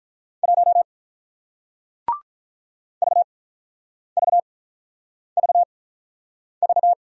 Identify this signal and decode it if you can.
Morse code, heard as a keyed tone, about 48 wpm, spaced as for 9 wpm, 700 Hz. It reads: 1 VVVV